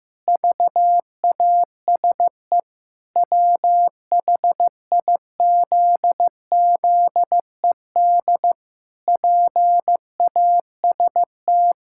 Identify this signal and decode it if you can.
Morse code, heard as a keyed tone, about 15 wpm, 700 Hz. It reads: VASE WHIZZED PAST
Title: Pangram Challenge v2 - 15wpm